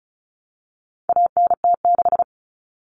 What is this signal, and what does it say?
Morse code, heard as a keyed tone, about 34 wpm, 700 Hz. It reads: ANT6
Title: Common Cut Numbers - 4-Digit 35wpm